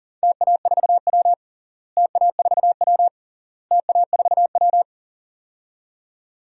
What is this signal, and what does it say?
Morse code, heard as a keyed tone, about 40 wpm, 700 Hz. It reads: TA4W TA4W TA4W
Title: DXCC-100 Call Signs - 4 Characters - Repeated 3x - 40wpm